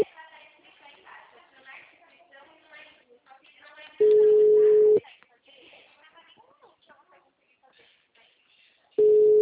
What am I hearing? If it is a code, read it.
Ringback tone (Europe, 425 Hz), with speech.
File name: out- -2004- - - .63741